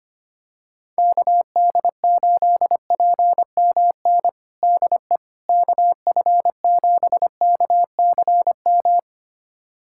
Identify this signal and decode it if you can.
Morse code, heard as a keyed tone, about 25 wpm, 700 Hz. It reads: KD8PMN DE KF7KCM
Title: USA Callsign DE Callsign - 6 Characters - 25wpm